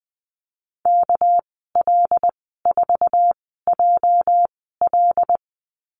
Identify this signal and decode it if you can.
Morse code, heard as a keyed tone, about 20 wpm, spaced as for 14 wpm, 700 Hz. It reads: KL4JL
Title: 231 Lesson 37g - Letter x - Call Signs 20wpm